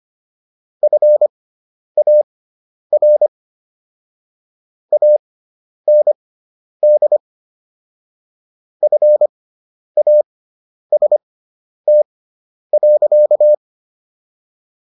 Morse code, heard as a keyed tone, about 25 wpm, spaced as for 10 wpm, 600 Hz. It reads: FAR AND FAST.